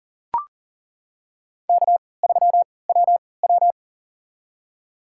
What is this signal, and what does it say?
Morse code, heard as a keyed tone, about 37 wpm, spaced as for 22 wpm, 700 Hz. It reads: K3WW
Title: USA Call Signs - 4 Characters - 3x Character Spacing - 40wpm